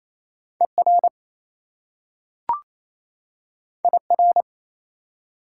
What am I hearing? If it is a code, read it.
Morse code, heard as a keyed tone, about 27 wpm, 690 Hz. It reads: ER IR
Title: Prefixes and Suffixes 28wpm